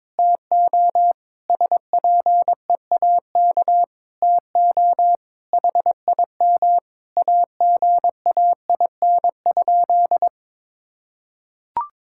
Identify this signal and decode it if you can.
Morse code, heard as a keyed tone, about 22 wpm, 700 Hz. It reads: TO SPEAK TO HIM AGAIN?
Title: Sentences from Top 1600 Words 22wpm